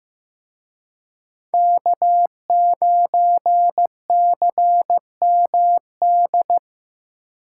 Morse code, heard as a keyed tone, about 18 wpm, 700 Hz. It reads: K9CMD
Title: Speed Racing - USA 1x3 Call Signs 15wpm